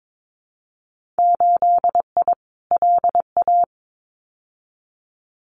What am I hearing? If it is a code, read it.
Morse code, heard as a keyed tone, about 22 wpm, 700 Hz. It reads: 8I LA
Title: Winter Field Day Exchanges Encore 22wpm